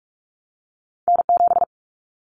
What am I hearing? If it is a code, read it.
Morse code, heard as a keyed tone, about 44 wpm, 700 Hz. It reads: N7
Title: Callsign Prefixes for Top 25 DXCC - Repeated 3x - 45wpm